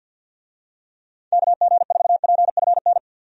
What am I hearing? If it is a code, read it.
Morse code, heard as a keyed tone, about 48 wpm, 700 Hz. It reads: KG4PFN
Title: USA Call Signs - 6 Characters - 50wpm